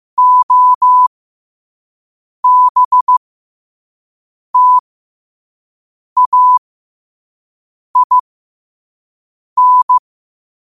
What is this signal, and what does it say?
Morse code, heard as a keyed tone, about 15 wpm, spaced as for 5 wpm, 1000 Hz. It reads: OBTAIN
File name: qotdfiles / 5wpm